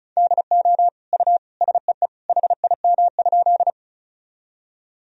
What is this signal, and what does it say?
Morse code, heard as a keyed tone, about 34 wpm, 700 Hz. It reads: DO U SEE HIM?
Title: Sentences from Top 100 Words and Abbreviations 35wpm